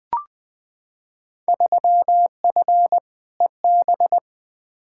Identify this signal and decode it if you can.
Morse code, heard as a keyed tone, about 20 wpm, 700 Hz. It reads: 3F EB
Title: Field Day Exchanges - 20wpm